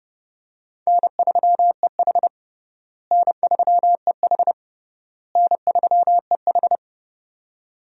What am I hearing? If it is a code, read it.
Morse code, heard as a keyed tone, about 32 wpm, 700 Hz. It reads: N3EH N3EH N3EH
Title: USA Call Signs - 4 Characters - Repeated 3x - 30wpm